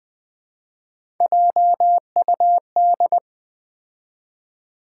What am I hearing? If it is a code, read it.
Morse code, heard as a keyed tone, about 20 wpm, 700 Hz. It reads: JUD
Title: Build A Word - 3 Letters At A Time - 8-Letter Words - 20wpm